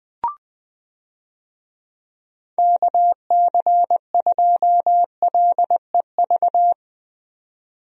Morse code, heard as a keyed tone, about 21 wpm, 700 Hz. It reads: KC2LEV
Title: USA Call Signs - 6 Characters - 20wpm